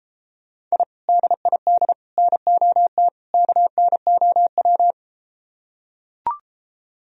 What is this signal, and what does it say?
Morse code, heard as a keyed tone, about 34 wpm, 700 Hz. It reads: I DID NOT KNOW